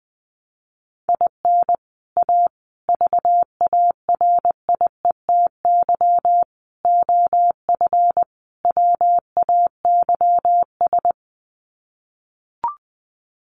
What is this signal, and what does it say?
Morse code, heard as a keyed tone, about 20 wpm, 700 Hz. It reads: IN A VARIETY OF WAYS
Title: Sets of 5 Words Encore 20wpm